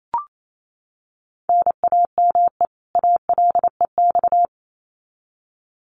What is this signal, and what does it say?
Morse code, heard as a keyed tone, about 28 wpm, 700 Hz. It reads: NAME ALEX
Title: Common QSO Phrases 28wpm